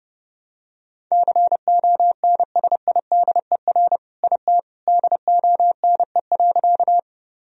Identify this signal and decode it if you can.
Morse code, heard as a keyed tone, about 30 wpm, 700 Hz. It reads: CONSIDER IT DONE.
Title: Sentences from Top 1000 Words - Review Part 4 - 30wpm